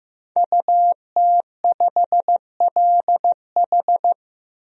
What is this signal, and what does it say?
Morse code, heard as a keyed tone, about 15 wpm, 700 Hz. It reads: UT5LH
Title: DXCC-100 Call Signs - 5 Characters - Repeated 3x - 15wpm